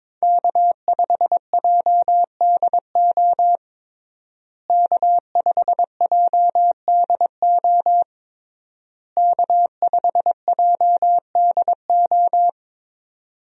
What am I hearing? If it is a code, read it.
Morse code, heard as a keyed tone, about 22 wpm, 700 Hz. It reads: K5JDO K5JDO K5JDO
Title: USA Call Signs - 5 Characters - Repeated 3x - 22wpm